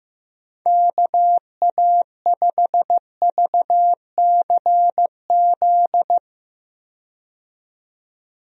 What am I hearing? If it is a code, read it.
Morse code, heard as a keyed tone, about 15 wpm, 700 Hz. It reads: KA5VCZ
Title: USA Call Signs - 6 Characters - 15wpm